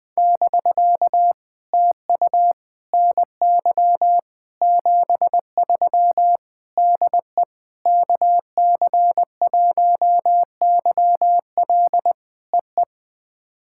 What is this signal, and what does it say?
Morse code, heard as a keyed tone, about 20 wpm, 700 Hz. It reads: <BK> TU NY 73 DE KC1YL EE
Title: POTA QSO Practice - 20wpm